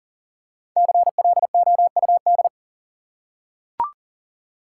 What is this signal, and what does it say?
Morse code, heard as a keyed tone, about 40 wpm, 700 Hz. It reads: CLOUD
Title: Five Letter Words - 40wpm